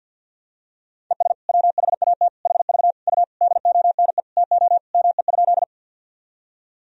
Morse code, heard as a keyed tone, about 48 wpm, 700 Hz. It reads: ES WHAT HV U DONE TO ME?
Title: Sentences from Top 500 Words and Abbreviations 50wpm